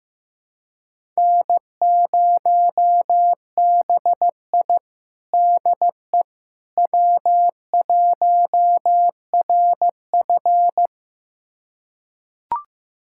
Morse code, heard as a keyed tone, about 16 wpm, 700 Hz. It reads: N0BI DE W1RF
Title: USA Callsign DE Callsign - 4 Characters - 15wpm